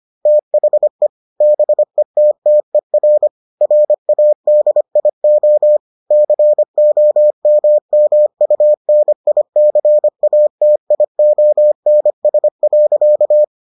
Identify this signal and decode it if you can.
Morse code, heard as a keyed tone, about 25 wpm, 600 Hz. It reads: THE BETTER RADIO COMMUNICATIONS.